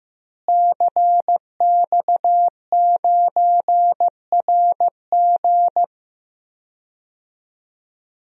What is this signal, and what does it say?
Morse code, heard as a keyed tone, about 15 wpm, 700 Hz. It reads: CX9RG
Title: DXCC-100 Call Signs - 5 Characters - Repeated 2x - 15wpm